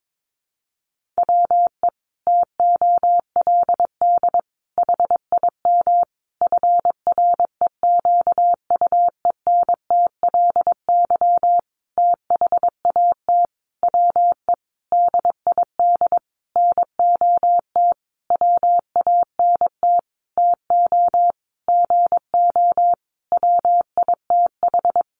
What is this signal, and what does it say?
Morse code, heard as a keyed tone, about 22 wpm, 700 Hz. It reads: WE TOLD HIM FREQUENTLY THAT WE DID NOT WANT TO GO WITH
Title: Sentences from Top 1300 Words 22wpm